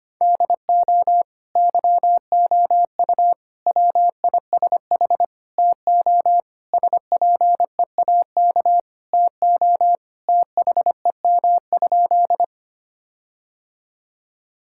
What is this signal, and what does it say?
Morse code, heard as a keyed tone, about 25 wpm, 700 Hz. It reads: DO YOU WISH TO SPEAK TO THEM?
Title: Sentences from Top 1600 Words 25wpm